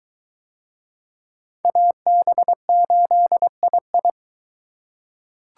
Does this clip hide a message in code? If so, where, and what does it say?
Yes: Morse code from 1.6 to 4.1 s, heard as a keyed tone, about 21 wpm, 700 Hz. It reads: AB8II